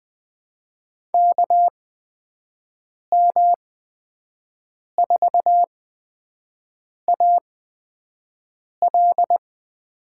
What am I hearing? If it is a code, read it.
Morse code, heard as a keyed tone, about 20 wpm, spaced as for 5 wpm, 700 Hz. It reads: KM4AL
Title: USA Call Signs - 8x Character Spacing - 20wpm